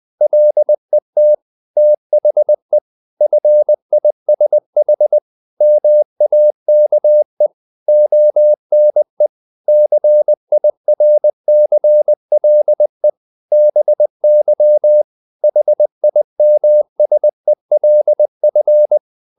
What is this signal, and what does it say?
Morse code, heard as a keyed tone, about 20 wpm, 600 Hz. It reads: LET THE FISH MAKE ONE CIRCLE BY HIMSELF
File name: the old man and the sea part 2 20wpm